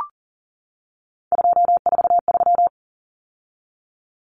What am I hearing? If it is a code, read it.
Morse code, heard as a keyed tone, about 40 wpm, 700 Hz. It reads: 243
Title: Numbers - 3-Digit 40wpm